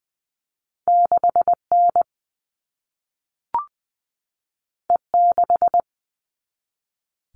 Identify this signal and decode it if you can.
Morse code, heard as a keyed tone, about 20 wpm, 700 Hz. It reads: 6N E6
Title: All Cut Numbers - 2-Digit 20wpm